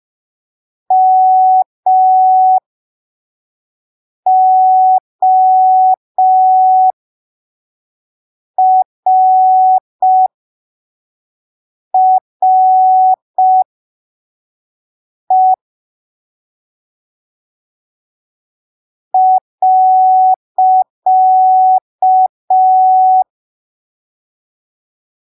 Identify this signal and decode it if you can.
Morse code, heard as a keyed tone, about 5 wpm, spaced as for 3 wpm, 750 Hz. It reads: MORRE .
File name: arquivos / texto / texto 20 05ppm